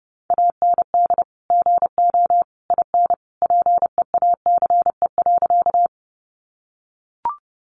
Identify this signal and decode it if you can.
Morse code, heard as a keyed tone, about 30 wpm, 700 Hz. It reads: AND GO IN PEACE.
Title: Sentences from Top 1500 Words 30wpm